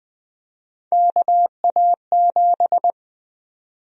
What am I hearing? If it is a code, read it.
Morse code, heard as a keyed tone, about 20 wpm, 700 Hz. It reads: KA7
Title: USA Call Sign Prefixes - 20wpm